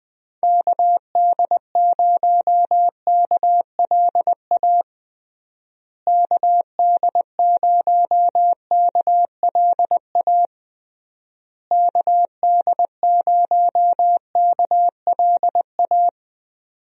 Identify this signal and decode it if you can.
Morse code, heard as a keyed tone, about 20 wpm, 700 Hz. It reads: KD0KLA KD0KLA KD0KLA
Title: USA Call Signs - 6 Characters - Repeated 3x - 20wpm